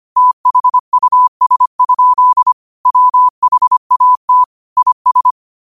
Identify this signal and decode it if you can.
Morse code, heard as a keyed tone, about 25 wpm, 1000 Hz. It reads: THUS? WHAT IS
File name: qotdfiles / 25wpm